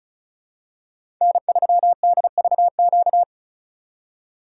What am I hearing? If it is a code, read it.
Morse code, heard as a keyed tone, about 34 wpm, 700 Hz. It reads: N3DVQ
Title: USA Call Signs - 2x Character Spacing - 35wpm